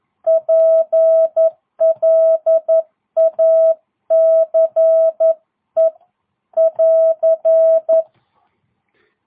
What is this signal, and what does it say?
Morse code, heard as a keyed tone, about 11 wpm, 660 Hz. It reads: PLACE <AR>